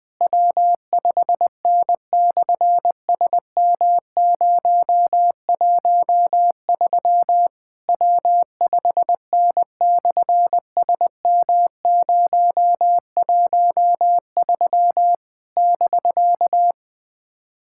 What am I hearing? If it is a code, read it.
Morse code, heard as a keyed tone, about 20 wpm, 700 Hz. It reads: W5N/SM013 W5N/SM013 <BK>
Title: SOTA QSO Practice 20wpm